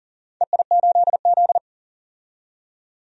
Morse code, heard as a keyed tone, about 40 wpm, 700 Hz. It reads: EI8Z
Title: DXCC-100 Call Signs - 4 Characters - 40wpm